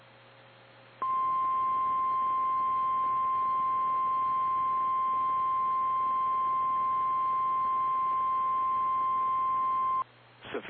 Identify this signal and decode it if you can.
Emergency Alert System (SAME) tones (the 1050 Hz attention tone), with speech. It is heard from 1.0 to 10.1 s.
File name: eas / KKPRKXPN / KKPRKXPN